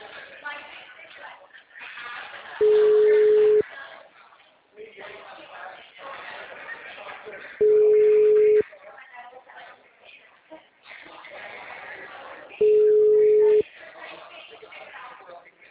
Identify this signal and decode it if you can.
Ringback tone (Europe, 425 Hz), with speech.